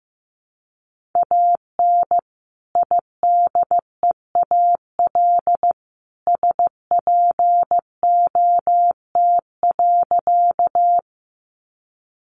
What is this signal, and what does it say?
Morse code, heard as a keyed tone, about 15 wpm, 700 Hz. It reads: AN IDEAL SPOT.
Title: Sentences from Top 1400 Words 15wpm